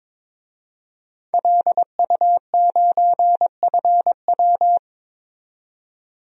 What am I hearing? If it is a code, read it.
Morse code, heard as a keyed tone, about 22 wpm, 700 Hz. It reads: LU9FW
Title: DXCC-100 Call Signs - 5 Characters - 22wpm